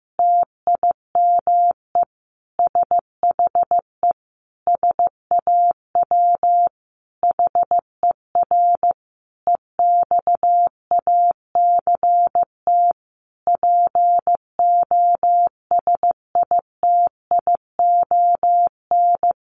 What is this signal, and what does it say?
Morse code, heard as a keyed tone, about 15 wpm, 700 Hz. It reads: TIME SHE SAW HER EXACT POSITION
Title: Sentences from Top 1300 Words 15wpm